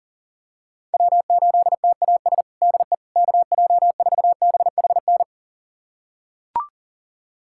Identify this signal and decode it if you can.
Morse code, heard as a keyed tone, about 40 wpm, 700 Hz. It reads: W8TAS DE KJ4BHN
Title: USA Callsign DE Callsign 40wpm